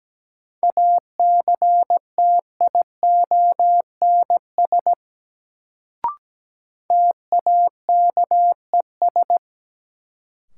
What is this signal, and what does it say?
Morse code, heard as a keyed tone, about 17 wpm, 700 Hz. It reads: ACTIONS TAKES
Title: Sentences from Top 1000 Words - Review Part 6 - New Word Drill - 17wpm